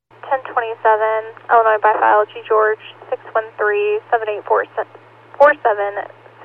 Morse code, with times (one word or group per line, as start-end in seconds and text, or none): none